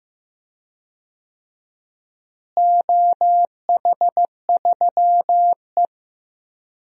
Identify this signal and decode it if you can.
Morse code, heard as a keyed tone, about 15 wpm, 700 Hz. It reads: OH3E
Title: DXCC-100 Call Signs - 4 Characters - Repeated 3x - 15wpm